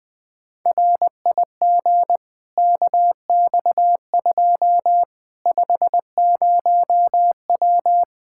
Morse code, heard as a keyed tone, about 20 wpm, 700 Hz. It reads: RIG KX2 50W
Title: K1USN SST QSO Practice 20wpm